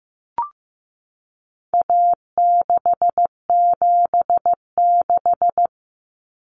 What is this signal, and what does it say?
Morse code, heard as a keyed tone, about 15 wpm, 700 Hz. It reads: A676
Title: Common Cut Numbers - 4-Digit 15wpm